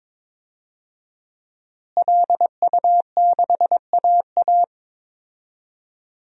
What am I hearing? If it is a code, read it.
Morse code, heard as a keyed tone, about 22 wpm, 700 Hz. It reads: LU6AA
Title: DXCC-100 Call Signs - 5 Characters - Repeated 3x - 22wpm